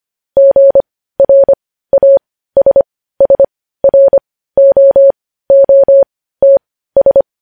Morse code, heard as a keyed tone, about 25 wpm, spaced as for 15 wpm, 550 Hz. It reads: GRASSROOTS
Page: news headlines in morse code at 15 wpm